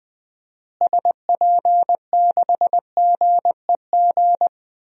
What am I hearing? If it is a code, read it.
Morse code, heard as a keyed tone, about 20 wpm, 700 Hz. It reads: SP6GEG
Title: DXCC-100 Call Signs - 6 Characters - Repeated 3x - 20wpm